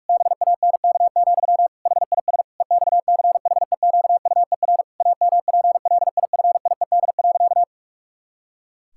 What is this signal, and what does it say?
Morse code, heard as a keyed tone, about 42 wpm, 700 Hz. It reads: BANK, HIS EXCHEQUER AMPLIFIED.
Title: Pangram Challenge v2 - 45wpm